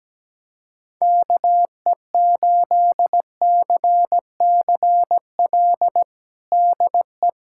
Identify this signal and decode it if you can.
Morse code, heard as a keyed tone, about 17 wpm, 700 Hz. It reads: KE8CCL DE
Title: USA Callsign DE Callsign - 6 Characters - 17wpm